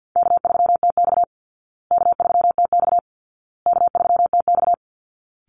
Morse code, heard as a keyed tone, about 48 wpm, 700 Hz. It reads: K3TX K3TX K3TX
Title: Easy Call Signs Repeated 3x - 50wpm